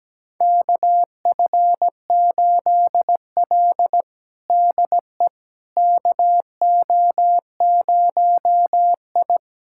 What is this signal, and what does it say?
Morse code, heard as a keyed tone, about 17 wpm, 700 Hz. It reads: KF8L DE KO0I
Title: USA Callsign DE Callsign - 4 Characters - 17wpm